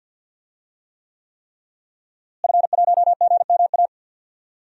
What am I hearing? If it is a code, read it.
Morse code, heard as a keyed tone, about 52 wpm, 700 Hz. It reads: F1GMA
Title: DXCC-100 Call Signs - 5 Characters - Repeated 3x - 50wpm